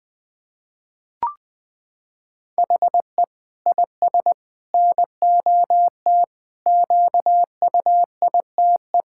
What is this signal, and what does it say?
Morse code, heard as a keyed tone, about 21 wpm, 700 Hz. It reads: HE IS NOT QUITE
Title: Sentences from Top 1600 Words 20wpm